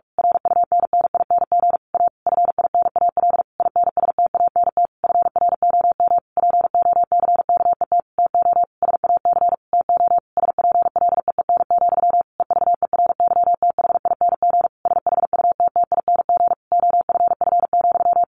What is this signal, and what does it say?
Morse code, heard as a keyed tone, about 44 wpm, 700 Hz. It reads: RUNNING A FINAL INSTANT FROM POCKET TO SAC TO SPLEEN, EVERYTHING SHUTTING OFF,
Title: A Sound Of Thunder 45wpm